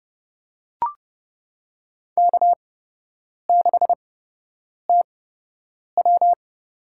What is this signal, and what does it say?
Morse code, heard as a keyed tone, about 30 wpm, spaced as for 8 wpm, 700 Hz. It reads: K6TW
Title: USA Call Signs - 8x Character Spacing - 30wpm